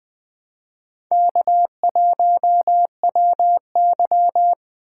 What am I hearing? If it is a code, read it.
Morse code, heard as a keyed tone, about 20 wpm, 700 Hz. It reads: K1WY
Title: Speed Racing - USA 1x2 Call Signs 17wpm